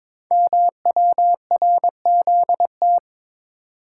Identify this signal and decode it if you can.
Morse code, heard as a keyed tone, about 22 wpm, 700 Hz. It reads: MWRZT
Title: Code Groups - 5 Letters - 22wpm